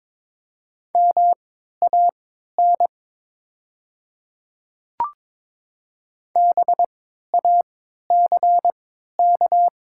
Morse code, heard as a keyed tone, about 22 wpm, spaced as for 12 wpm, 690 Hz. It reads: MAN BACK
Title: 3x Character Spacing - Top 200 Words - 22wpm